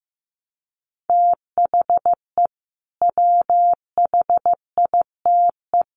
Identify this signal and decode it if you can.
Morse code, heard as a keyed tone, about 15 wpm, 700 Hz. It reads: THE WHITE
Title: Sets of 3 Words 15wpm